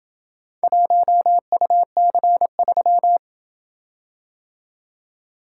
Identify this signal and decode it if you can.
Morse code, heard as a keyed tone, about 27 wpm, 700 Hz. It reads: 1UC3